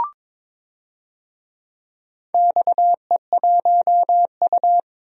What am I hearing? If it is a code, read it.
Morse code, heard as a keyed tone, about 22 wpm, 700 Hz. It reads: XE1U
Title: DXCC-100 Call Signs - 4 Characters - 22wpm